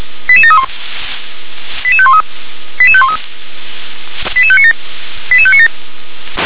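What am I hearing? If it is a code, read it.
Five-tone selective call (ZVEI), heard as quick runs of five tones, tones of about 70 ms. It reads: 88731 88513 88513 88597 88597
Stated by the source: ZVEI five-tone selective call 88510, received on SDR.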